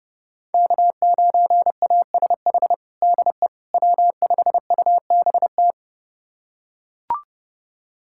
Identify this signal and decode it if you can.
Morse code, heard as a keyed tone, about 32 wpm, 700 Hz. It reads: K9ASH DE W5UBT
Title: USA Callsign DE Callsign - 5 Characters - 30wpm